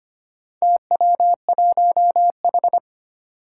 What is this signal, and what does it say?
Morse code, heard as a keyed tone, about 25 wpm, 700 Hz. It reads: TW1H
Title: Four Challenging Letter-Number Combinations 25wpm